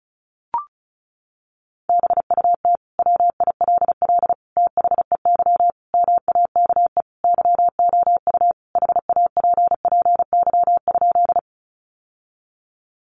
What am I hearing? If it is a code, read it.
Morse code, heard as a keyed tone, about 34 wpm, 700 Hz. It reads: BUT WILL THEY MAKE YOU HAPPY?
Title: Sentences from Top 1000 Words 35wpm